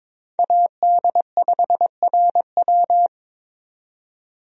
Morse code, heard as a keyed tone, about 22 wpm, 700 Hz. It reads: AD5RW
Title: USA Call Signs - 5 Characters - 22wpm